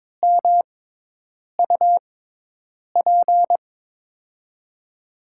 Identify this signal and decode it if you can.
Morse code, heard as a keyed tone, about 22 wpm, spaced as for 8 wpm, 700 Hz. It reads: MUP
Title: USA Call Signs - 5 Characters - 6x Character Spacing - 22wpm